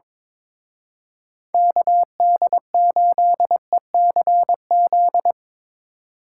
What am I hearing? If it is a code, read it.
Morse code, heard as a keyed tone, about 24 wpm, 700 Hz. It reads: KD8ECZ
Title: Speed Racing - USA 2x3 Call Signs 22wpm